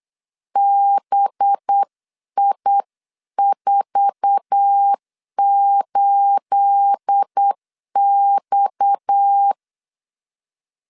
Morse code, heard as a keyed tone, about 8 wpm, 800 Hz. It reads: BI48X
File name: CW / DET / 1 9 1500C S C 8wpm 5653